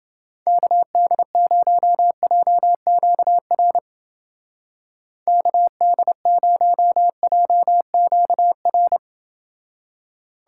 Morse code, heard as a keyed tone, about 30 wpm, 700 Hz. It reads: KD0JQR KD0JQR